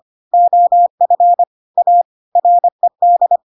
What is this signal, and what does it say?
Morse code, heard as a keyed tone, about 25 wpm, 700 Hz. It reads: OF A RED
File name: Pinocchio / Pinocchio ch2 25 25 ik1hge